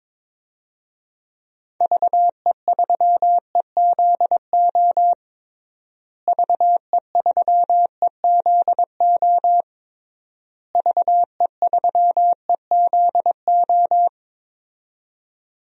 Morse code, heard as a keyed tone, about 22 wpm, 700 Hz. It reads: VE3EZO VE3EZO VE3EZO
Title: DXCC-100 Call Signs - 6 Characters - Repeated 3x - 22wpm